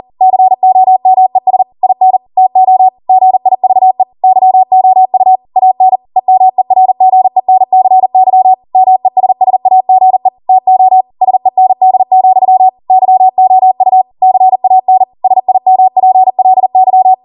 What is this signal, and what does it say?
Morse code, heard as a keyed tone, about 40 wpm, 750 Hz. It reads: COMES IN TO GIVE YOU AN EMERGENCY MESSAGE TO SEND, YOU CAN SIMPLY
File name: Morse / 40WPM